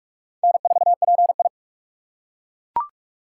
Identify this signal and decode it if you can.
Morse code, heard as a keyed tone, about 39 wpm, 700 Hz. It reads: N4PI